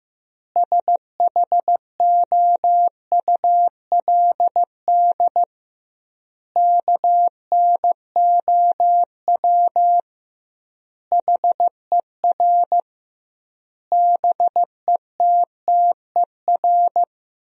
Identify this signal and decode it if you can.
Morse code, heard as a keyed tone, about 15 wpm, 700 Hz. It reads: SHOULD KNOW HER BETTER
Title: 2x Word-Spacing - Sentences from Top 400 Words - 15wpm